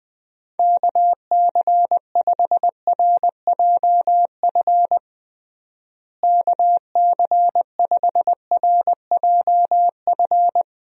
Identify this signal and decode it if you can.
Morse code, heard as a keyed tone, about 20 wpm, 700 Hz. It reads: KC5RJF KC5RJF
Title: USA Call Signs - 6 Characters - Repeated 2x - 20wpm